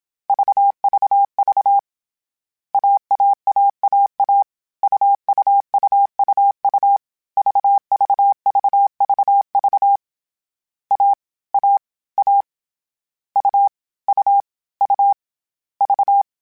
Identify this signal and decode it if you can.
Morse code, heard as a keyed tone, about 27 wpm, 800 Hz. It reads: VVV AAAAA UUUUU VVVVV A A A U U U V